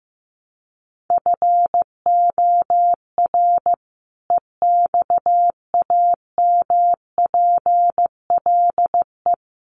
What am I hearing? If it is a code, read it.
Morse code, heard as a keyed tone, about 15 wpm, 700 Hz. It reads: FOR EXAMPLE
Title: Sets of 2 Words 15wpm